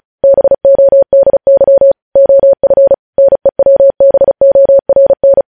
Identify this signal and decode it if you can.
Morse code, heard as a keyed tone, about 36 wpm, 550 Hz. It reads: BODY OF NEWBORN
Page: News Headlines in Morse Code at 35 WPM